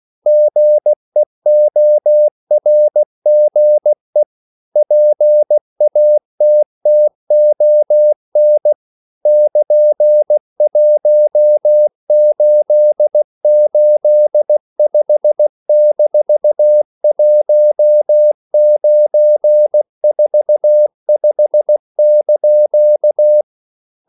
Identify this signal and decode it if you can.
Morse code, heard as a keyed tone, about 16 wpm, 600 Hz. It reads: GEORGE PATTON <KN>1885-1945)